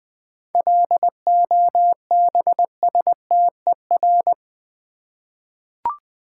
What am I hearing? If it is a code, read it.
Morse code, heard as a keyed tone, about 20 wpm, 690 Hz. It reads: LOBSTER
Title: Easy Words with Hints - 20wpm